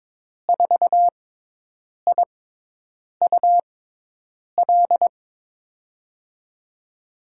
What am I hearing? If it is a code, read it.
Morse code, heard as a keyed tone, about 22 wpm, spaced as for 8 wpm, 700 Hz. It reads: 4IUL
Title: USA Call Signs - 6 Characters - 6x Character Spacing - 22wpm